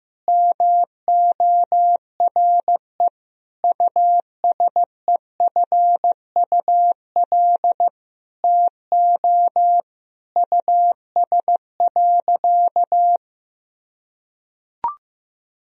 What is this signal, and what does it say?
Morse code, heard as a keyed tone, about 15 wpm, 700 Hz. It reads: MORE USEFUL TO US.
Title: Sentences from Top 1000 Words - Review Part 4 - 15wpm